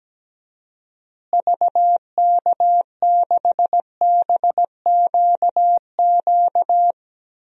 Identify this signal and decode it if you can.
Morse code, heard as a keyed tone, about 17 wpm, 700 Hz. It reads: VK6BQQ
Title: DXCC-100 Call Signs - 6 Characters - Repeated 2x - 17wpm